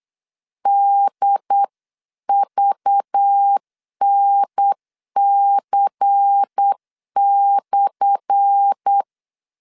Morse code, heard as a keyed tone, about 8 wpm, 800 Hz. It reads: DVNC/
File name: LEZIONI / 10 SIGNS / 10B INT 8wpm 3534